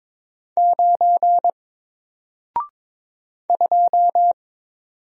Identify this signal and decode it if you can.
Morse code, heard as a keyed tone, about 22 wpm, 700 Hz. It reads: 9 2
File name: Single-Letter-Number-22wpm